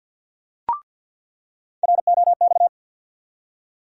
Morse code, heard as a keyed tone, about 47 wpm, 700 Hz. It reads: ROX
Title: USA Call Sign Postfixes - 3 Letters 50wpm